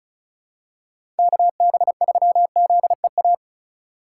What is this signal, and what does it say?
Morse code, heard as a keyed tone, about 36 wpm, 700 Hz. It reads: KB3ZEA
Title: USA Call Signs - 2x Character Spacing - 35wpm